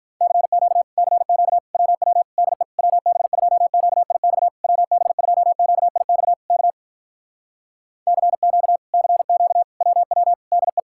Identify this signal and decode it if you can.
Morse code, heard as a keyed tone, about 52 wpm, 700 Hz. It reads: CQ CQ WW DE WB2QIX WB2QIX K CQ CQ WW DE